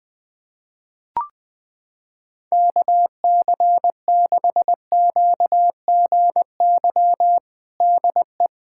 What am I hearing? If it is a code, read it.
Morse code, heard as a keyed tone, about 21 wpm, 700 Hz. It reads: KC6QGY DE
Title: USA Callsign DE Callsign - 6 Characters - 20wpm